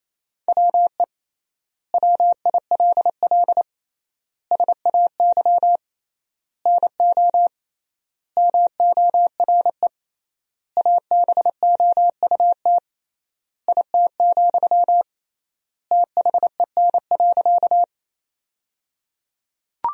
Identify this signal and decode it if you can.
Morse code, heard as a keyed tone, about 28 wpm, 700 Hz. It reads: WE WILL SAY NO MORE ABOUT IT, THEN.
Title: 3x Word-Spacing - Sentences from Top 200 Words - 28wpm